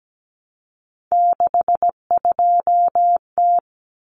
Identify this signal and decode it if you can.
Morse code, heard as a keyed tone, about 17 wpm, 700 Hz. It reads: 62T
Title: Common Cut Numbers - 3-Digit 17wpm